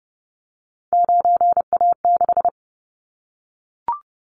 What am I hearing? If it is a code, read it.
Morse code, heard as a keyed tone, about 30 wpm, 700 Hz. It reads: 9A6
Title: Callsign Prefixes for Top 25 DXCC - Repeated 3x - 30wpm